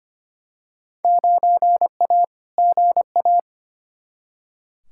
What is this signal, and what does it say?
Morse code, heard as a keyed tone, about 25 wpm, 700 Hz. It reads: 9A GA
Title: Field Day Exchanges Encore - 25wpm